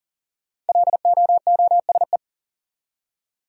Morse code, heard as a keyed tone, about 37 wpm, 700 Hz. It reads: LOOSE
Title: Five Letter Words - 40wpm